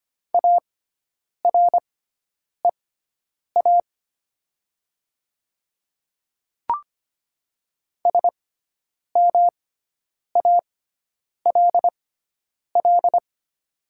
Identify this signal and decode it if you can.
Morse code, heard as a keyed tone, about 25 wpm, spaced as for 9 wpm, 700 Hz. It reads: AREA SMALL
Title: 6x Character Spacing - Top 500 Words - 25wpm